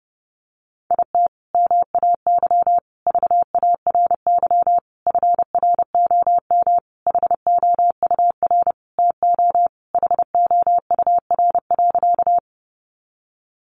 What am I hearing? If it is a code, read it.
Morse code, heard as a keyed tone, about 30 wpm, 700 Hz. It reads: IT MAY VARY FROM HOUR TO HOUR.
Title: Sentences from Top 1200 Words 30wpm